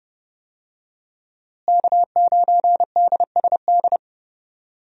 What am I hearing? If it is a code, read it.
Morse code, heard as a keyed tone, about 30 wpm, 700 Hz. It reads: K9DSD
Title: USA Call Signs - 5 Characters - Repeated 3x - 30wpm